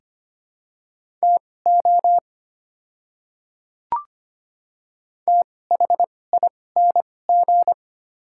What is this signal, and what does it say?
Morse code, heard as a keyed tone, about 25 wpm, spaced as for 18 wpm, 690 Hz. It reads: TO THING
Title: 2x Character Spacing - Top 100 Words - 25wpm